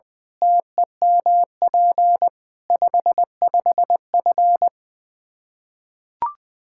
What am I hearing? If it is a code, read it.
Morse code, heard as a keyed tone, about 21 wpm, 700 Hz. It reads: TEMP 55F
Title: K1USN SST QSO Practice 20wpm